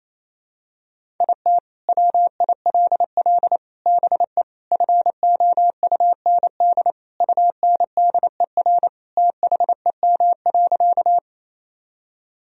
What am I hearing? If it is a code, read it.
Morse code, heard as a keyed tone, about 28 wpm, 700 Hz. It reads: IT WILL BE FOUND UNDER THEM.
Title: Sentences from Top 1000 Words - Review Part 1 - 28wpm